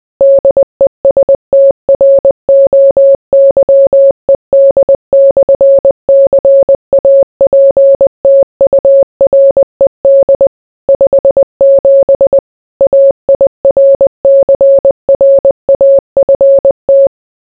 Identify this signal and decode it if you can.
Morse code, heard as a keyed tone, about 20 wpm, 550 Hz. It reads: DESTROYED/CAPTURED 57 AIRCRAFT